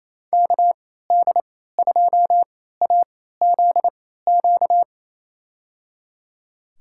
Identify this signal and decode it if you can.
Morse code, heard as a keyed tone, about 28 wpm, spaced as for 16 wpm, 700 Hz. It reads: KD2AZQ
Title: USA Call Signs - 6 Characters - 3x Character Spacing - 28wpm